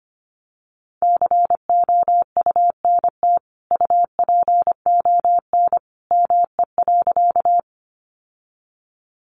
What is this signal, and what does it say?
Morse code, heard as a keyed tone, about 25 wpm, 700 Hz. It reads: COUNT UPON ME.
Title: Sentences from Top 1500 Words 25wpm